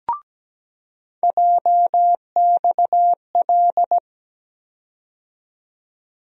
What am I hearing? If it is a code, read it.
Morse code, heard as a keyed tone, about 17 wpm, 700 Hz. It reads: JXL
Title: USA Call Sign Postfixes - 3 Letters 17wpm